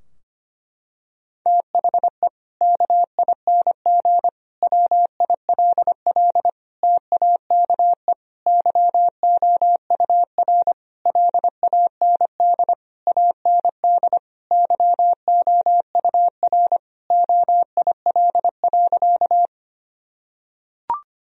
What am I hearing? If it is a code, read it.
Morse code, heard as a keyed tone, about 25 wpm, 700 Hz. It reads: THE KING WILL TAKE YOUR LAND AND YOUR OIL.
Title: Sentences from Top 1000 Words - Review Part 4 - 25wpm